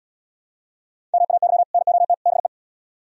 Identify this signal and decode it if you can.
Morse code, heard as a keyed tone, about 74 wpm, 700 Hz. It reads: MAY NOT BE